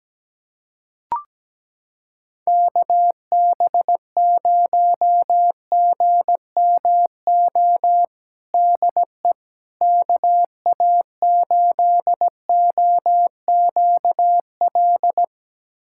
Morse code, heard as a keyed tone, about 17 wpm, 700 Hz. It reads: KB0GMO DE KA8OQL
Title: USA Callsign DE Callsign 17wpm